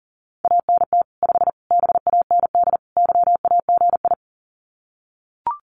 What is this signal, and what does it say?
Morse code, heard as a keyed tone, about 40 wpm, 700 Hz. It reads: ANT 5 BAND YAGI
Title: Common QSO Phrases 40wpm